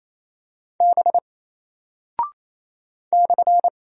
Morse code, heard as a keyed tone, about 28 wpm, 700 Hz. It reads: B /
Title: Single Character (Letter:Number:Punctuation:Prosign) Alt 28wpm